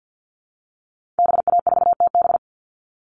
Morse code, heard as a keyed tone, about 50 wpm, 700 Hz. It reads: BA4TB
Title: Call Signs Repeated 2x - 50wpm